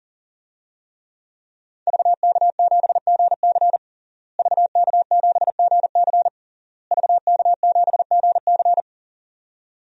Morse code, heard as a keyed tone, about 40 wpm, 700 Hz. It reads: VK7GC VK7GC VK7GC
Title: DXCC-100 Call Signs - 5 Characters - Repeated 3x - 40wpm